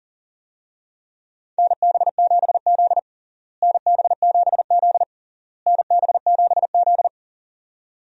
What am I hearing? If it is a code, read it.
Morse code, heard as a keyed tone, about 37 wpm, 700 Hz. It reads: NB7Z NB7Z NB7Z
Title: USA Call Signs - 4 Characters - Repeated 3x - 40wpm